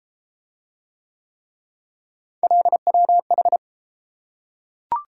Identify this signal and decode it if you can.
Morse code, heard as a keyed tone, about 30 wpm, 700 Hz. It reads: LWH